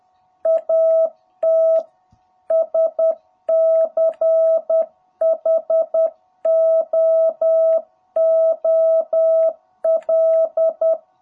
Morse code, heard as a keyed tone, about 10 wpm, 660 Hz. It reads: AT SCHOOL
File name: jh8cbh / 2021cw